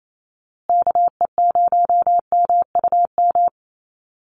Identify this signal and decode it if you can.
Morse code, heard as a keyed tone, about 28 wpm, 700 Hz. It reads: KE0MUM
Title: USA Call Signs Repeated 3x 28wpm